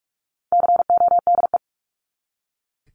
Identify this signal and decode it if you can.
Morse code, heard as a keyed tone, about 44 wpm, 700 Hz. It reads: CODE
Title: Top 1501-1750 Words 45wpm